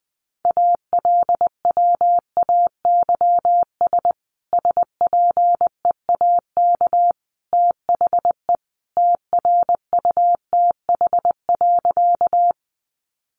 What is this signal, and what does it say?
Morse code, heard as a keyed tone, about 19 wpm, 700 Hz. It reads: ALWAYS SPEAK THE TRUTH.
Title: Sentences from Top 1100 Words 20wpm